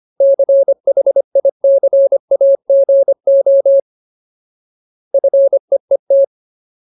Morse code, heard as a keyed tone, about 25 wpm, 550 Hz. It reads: CHICAGO FEET
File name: Ultimate Word List / 25WPM Ultimate